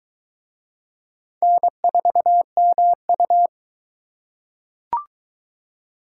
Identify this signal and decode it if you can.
Morse code, heard as a keyed tone, about 22 wpm, 700 Hz. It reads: N4MU